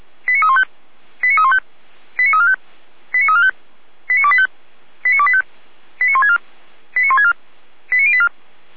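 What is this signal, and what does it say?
Five-tone selective call (ZVEI), heard as quick runs of five tones, tones of about 70 ms. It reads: 89315 89315 89345 89345 89285 89285 89164 89164 89084
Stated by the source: Five-tone ZVEI selective call received by SDR, from the server of Feuerwehr Orlamünde (fire brigade).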